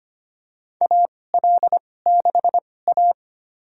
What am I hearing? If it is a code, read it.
Morse code, heard as a keyed tone, about 25 wpm, spaced as for 18 wpm, 700 Hz. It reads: AL6A
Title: USA Call Signs - 4 Characters - 2x Character Spacing - 25wpm